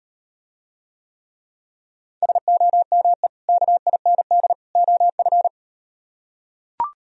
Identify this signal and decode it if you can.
Morse code, heard as a keyed tone, about 38 wpm, 700 Hz. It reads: SOME KIND OF